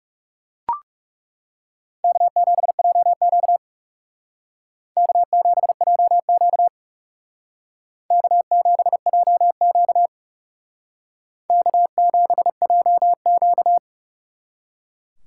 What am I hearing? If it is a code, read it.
Morse code, heard as a keyed tone, about 38 wpm, 700 Hz. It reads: K7JQ K7JQ K7JQ K7JQ